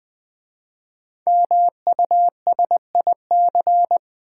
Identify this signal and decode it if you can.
Morse code, heard as a keyed tone, about 20 wpm, 700 Hz. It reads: MUSIC
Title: Easy Words with Hints - 20wpm